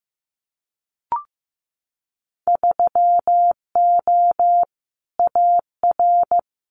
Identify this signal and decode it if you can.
Morse code, heard as a keyed tone, about 15 wpm, 700 Hz. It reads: 3O AR
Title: Winter Field Day Exchanges Encore 15wpm